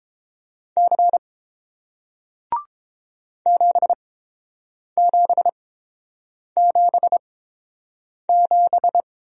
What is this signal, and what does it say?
Morse code, heard as a keyed tone, about 30 wpm, spaced as for 7 wpm, 700 Hz. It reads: C 7777